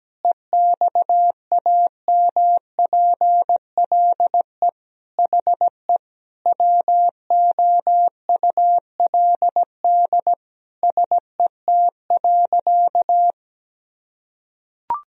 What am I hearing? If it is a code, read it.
Morse code, heard as a keyed tone, about 17 wpm, 700 Hz. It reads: EXAMPLE HE WOULD SET.
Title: Sentences from Top 1000 Words - Review Part 1 - 17wpm